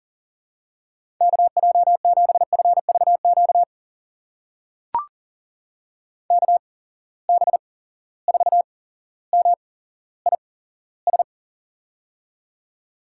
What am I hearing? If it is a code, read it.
Morse code, heard as a keyed tone, about 37 wpm, spaced as for 17 wpm, 700 Hz. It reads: KJ7FVQ KB4MIS
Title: USA Call Signs - 6 Characters - 8x Character Spacing - 40wpm